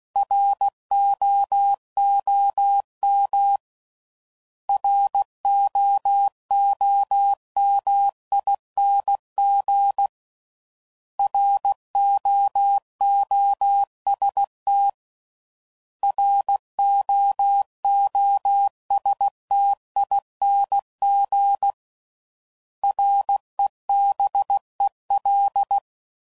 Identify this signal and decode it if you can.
Morse code, heard as a keyed tone, about 16 wpm, 800 Hz. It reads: ROOM ROOMING ROOST ROOSTING REBEL
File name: Prefixes / By First Letter / Rs